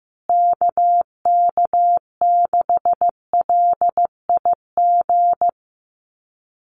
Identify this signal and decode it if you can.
Morse code, heard as a keyed tone, about 15 wpm, 700 Hz. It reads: KK6LIG
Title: USA Call Signs Repeated 3x 15wpm